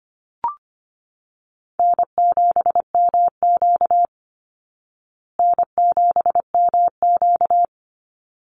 Morse code, heard as a keyed tone, about 25 wpm, 700 Hz. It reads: N7MQ N7MQ
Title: Easy Call Signs Repeated 2x - 25wpm